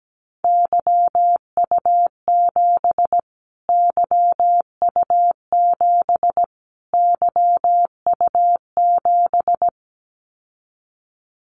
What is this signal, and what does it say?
Morse code, heard as a keyed tone, about 17 wpm, 700 Hz. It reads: YU7 YU7 YU7
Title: Callsign Prefixes for Top 25 DXCC - Repeated 3x - 17wpm